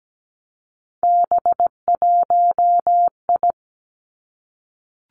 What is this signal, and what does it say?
Morse code, heard as a keyed tone, about 17 wpm, 700 Hz. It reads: B1I
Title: Three Challenging Letter-Number Combinations 17wpm